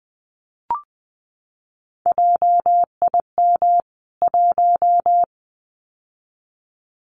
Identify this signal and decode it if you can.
Morse code, heard as a keyed tone, about 20 wpm, 700 Hz. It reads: JIM 1
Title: CWT Exchange 20wpm